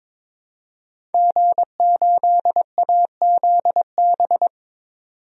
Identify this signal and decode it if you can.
Morse code, heard as a keyed tone, about 21 wpm, 700 Hz. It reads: G8AZB
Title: DXCC-100 Call Signs - 5 Characters - 22wpm